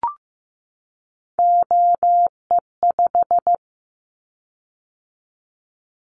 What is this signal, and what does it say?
Morse code, heard as a keyed tone, about 15 wpm, 700 Hz. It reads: OE5
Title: Callsign Prefixes for Top 25 DXCC - 15wpm